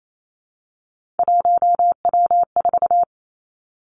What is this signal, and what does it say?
Morse code, heard as a keyed tone, about 28 wpm, 700 Hz. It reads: 1W4
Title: Three Challenging Letter-Number Combinations 28wpm